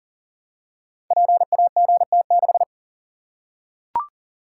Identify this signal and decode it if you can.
Morse code, heard as a keyed tone, about 40 wpm, 700 Hz. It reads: PAGT6
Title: Code Groups - 5 Letter:Numbers - 40wpm